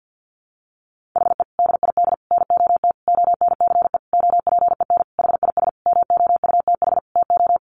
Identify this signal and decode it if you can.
Morse code, heard as a keyed tone, about 50 wpm, 700 Hz. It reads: HE DID NOT ONCE OPEN HIS MOUTH TO
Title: Sentences from Top 1500 Words 50wpm